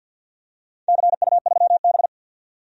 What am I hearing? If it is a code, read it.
Morse code, heard as a keyed tone, about 53 wpm, 700 Hz. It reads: CU3B
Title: DXCC-100 Call Signs - 4 Characters - 50wpm